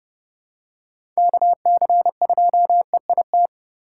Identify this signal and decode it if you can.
Morse code, heard as a keyed tone, about 32 wpm, 700 Hz. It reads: KC2EIT